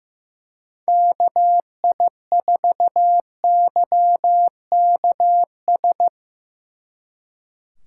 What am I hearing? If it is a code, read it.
Morse code, heard as a keyed tone, about 17 wpm, 700 Hz. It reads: KI4YKS
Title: Speed Racing - USA 2x3 Call Signs 15wpm